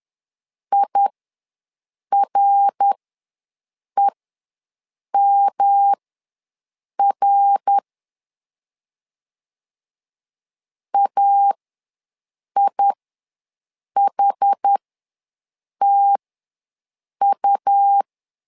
Morse code, heard as a keyed tone, about 11 wpm, spaced as for 6 wpm, 800 Hz. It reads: IREMR AIHTU
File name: LEZIONI / 3 MWRO / 3 MWR ws 10wpm 2151